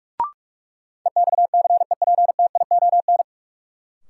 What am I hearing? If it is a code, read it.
Morse code, heard as a keyed tone, about 46 wpm, 700 Hz. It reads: EXCEPTION
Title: Sets of 4 Words - New Word Drill - 45wpm